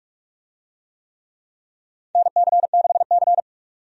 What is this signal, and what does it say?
Morse code, heard as a keyed tone, about 47 wpm, 700 Hz. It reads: NC6C
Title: USA Call Signs - 4 Characters - 45wpm